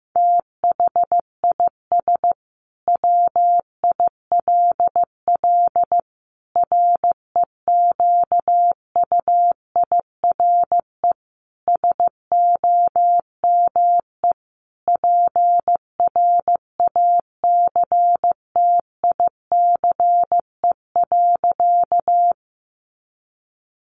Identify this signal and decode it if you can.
Morse code, heard as a keyed tone, about 15 wpm, 700 Hz. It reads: THIS WILL REQUIRE SOME PRACTICE.
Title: Sentences from Top 900 Words 15wpm